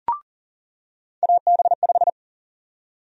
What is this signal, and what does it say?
Morse code, heard as a keyed tone, about 40 wpm, 700 Hz. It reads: AB5
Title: USA Call Sign Prefixes - 40wpm